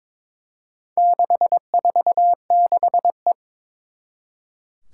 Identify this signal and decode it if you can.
Morse code, heard as a keyed tone, about 22 wpm, 700 Hz. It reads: 646E
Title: Four Challenging Letter-Number Combinations 22wpm